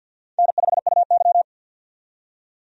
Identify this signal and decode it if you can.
Morse code, heard as a keyed tone, about 48 wpm, 700 Hz. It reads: N5UY
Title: USA Call Signs - 4 Characters - 50wpm